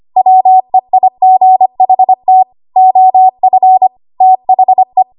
Morse code, heard as a keyed tone, about 25 wpm, 750 Hz. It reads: WEIGHT OF THE
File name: Archive / 25 WPM / 25WPM